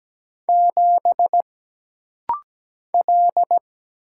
Morse code, heard as a keyed tone, about 17 wpm, 700 Hz. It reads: 7 L
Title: Single Letter-Number - Rapid-Fire 17wpm